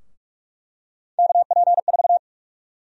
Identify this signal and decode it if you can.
Morse code, heard as a keyed tone, about 47 wpm, 700 Hz. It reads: KP4
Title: USA Call Sign Prefixes - 45wpm